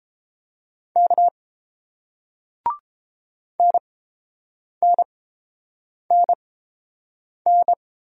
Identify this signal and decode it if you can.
Morse code, heard as a keyed tone, about 30 wpm, spaced as for 7 wpm, 700 Hz. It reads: K NNNN